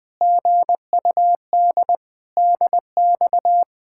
Morse code, heard as a keyed tone, about 20 wpm, 700 Hz. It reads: GUD DX
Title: Speed-Racing Common QSO Phrases - 17wpm